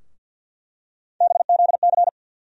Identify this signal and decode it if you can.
Morse code, heard as a keyed tone, about 50 wpm, 700 Hz. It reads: BZC
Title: USA Call Sign Postfixes - 3 Letters 50wpm